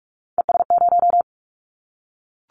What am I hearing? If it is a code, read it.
Morse code, heard as a keyed tone, about 44 wpm, 700 Hz. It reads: ES0
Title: Callsign Prefixes for Top 100 DXCC - 45wpm